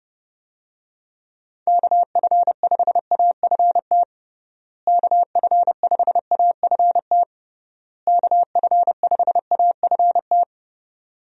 Morse code, heard as a keyed tone, about 30 wpm, 700 Hz. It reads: KF5AFT KF5AFT KF5AFT
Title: USA Call Signs - 6 Characters - Repeated 3x - 30wpm